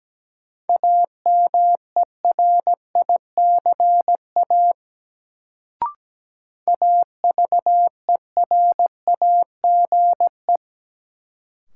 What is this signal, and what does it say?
Morse code, heard as a keyed tone, about 17 wpm, 700 Hz. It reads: AMERICA AVERAGE
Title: Sets of 5 Words - New Word Drill - 17wpm